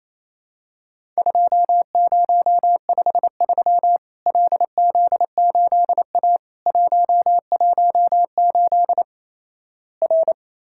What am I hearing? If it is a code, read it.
Morse code, heard as a keyed tone, about 28 wpm, 700 Hz. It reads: 2053 LZ8A 118 R
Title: Worked All Europe DX Contest - QTC Exchange - 28wpm